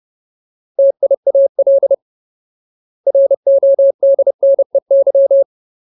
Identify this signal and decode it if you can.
Morse code, heard as a keyed tone, about 30 wpm, 550 Hz. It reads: TIAL RODNEY
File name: Ultimate Word List / 30WPM Ultimate